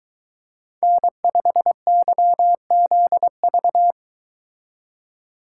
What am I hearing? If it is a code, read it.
Morse code, heard as a keyed tone, about 23 wpm, 700 Hz. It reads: N5YZV